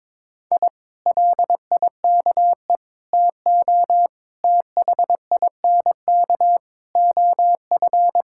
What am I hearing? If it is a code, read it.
Morse code, heard as a keyed tone, about 22 wpm, 700 Hz. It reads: I LIKE TO THINK OF
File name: practice / 22wpm / Sentences with 8-Word Phrases - 22wpm